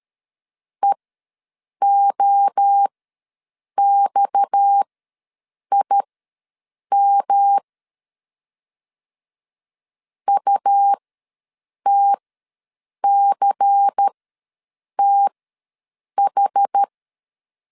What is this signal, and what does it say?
Morse code, heard as a keyed tone, about 13 wpm, spaced as for 7 wpm, 800 Hz. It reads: EOXIM UTCTH
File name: LEZIONI / 8 YJ45 / 8 YJ ws 12wpm 3342